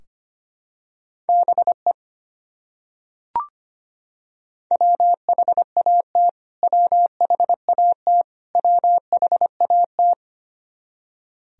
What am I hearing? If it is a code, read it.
Morse code, heard as a keyed tone, about 25 wpm, 700 Hz. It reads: BE WHAT WHAT WHAT
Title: Top 100 Words Repeated 25wpm